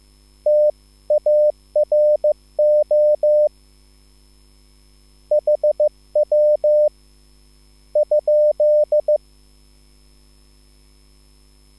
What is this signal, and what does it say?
Morse code, heard as a keyed tone, about 15 wpm, 600 Hz. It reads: TARO HW ?
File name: QSL / CW lesson